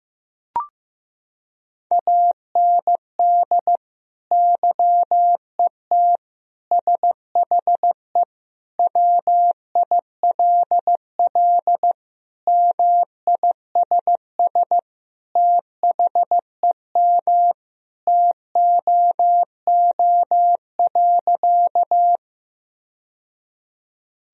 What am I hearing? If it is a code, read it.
Morse code, heard as a keyed tone, about 15 wpm, 700 Hz. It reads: AND YET SHE WILL MISS THEM TOO.
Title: Sentences from Top 1000 Words - Review Part 3 - 15wpm